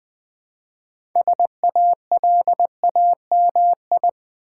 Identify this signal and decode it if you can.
Morse code, heard as a keyed tone, about 20 wpm, 700 Hz. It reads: SALAMI
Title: Easy Words with Hints - 20wpm